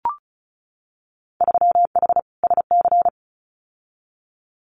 Morse code, heard as a keyed tone, about 34 wpm, 700 Hz. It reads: 3H SC
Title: Winter Field Day Exchanges 35wpm